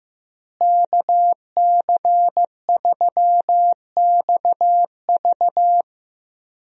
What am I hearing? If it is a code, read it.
Morse code, heard as a keyed tone, about 15 wpm, 700 Hz. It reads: KC3XV
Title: USA Call Signs - 5 Characters - Repeated 2x - 15wpm